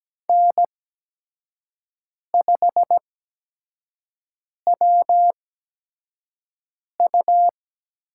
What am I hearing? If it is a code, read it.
Morse code, heard as a keyed tone, about 17 wpm, spaced as for 5 wpm, 700 Hz. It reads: N5WU
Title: USA Call Signs - 8x Character Spacing - 17wpm